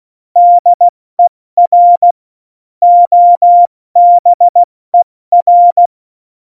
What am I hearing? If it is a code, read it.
Morse code, heard as a keyed tone, about 16 wpm, 700 Hz. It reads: DER OBER